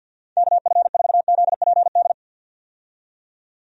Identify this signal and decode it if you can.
Morse code, heard as a keyed tone, about 50 wpm, 700 Hz. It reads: KF4ZPD
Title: USA Call Signs - 6 Characters - 50wpm